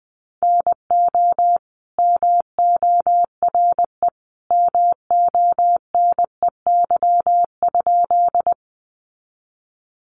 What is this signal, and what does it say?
Morse code, heard as a keyed tone, about 20 wpm, 700 Hz. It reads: NO MORE MONEY?
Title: Sentences from Top 400 Words 20wpm